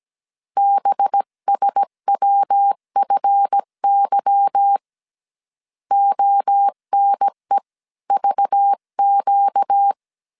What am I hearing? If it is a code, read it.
Morse code, heard as a keyed tone, about 17 wpm, 800 Hz. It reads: BSWFY ONEVQ
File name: 1500C S B 16wpm